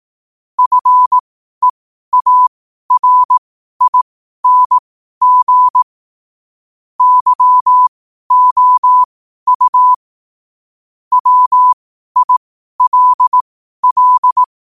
Morse code, heard as a keyed tone, about 18 wpm, spaced as for 12 wpm, 1000 Hz. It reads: FEARING YOU WILL
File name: qotdfiles / 12wpm